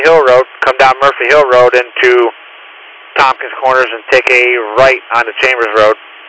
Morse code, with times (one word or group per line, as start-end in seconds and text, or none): none